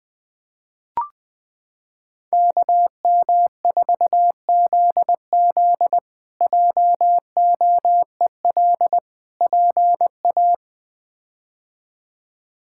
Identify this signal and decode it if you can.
Morse code, heard as a keyed tone, about 20 wpm, 700 Hz. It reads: KM4ZZ JOEL PA
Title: K1USN SST QSO Practice - 20wpm